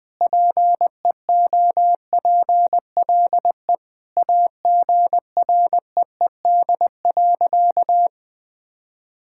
Morse code, heard as a keyed tone, about 20 wpm, 700 Hz. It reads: PEOPLE AGREED.
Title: Sentences with 3-Word Phrases - 20wpm